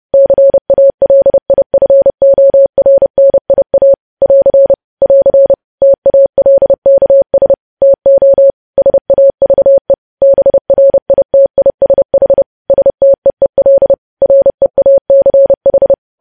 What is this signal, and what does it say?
Morse code, heard as a keyed tone, about 30 wpm, 550 Hz. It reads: CALIFORNIA <AR> <AR> TALKS TO SAVE BRITISH STEEL REACH
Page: news headlines in morse code at 30 wpm